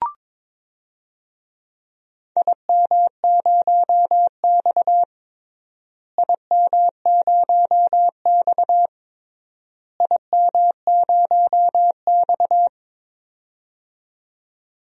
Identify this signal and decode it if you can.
Morse code, heard as a keyed tone, about 21 wpm, 700 Hz. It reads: IM0X IM0X IM0X
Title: DXCC-100 Call Signs - 4 Characters - Repeated 3x - 22wpm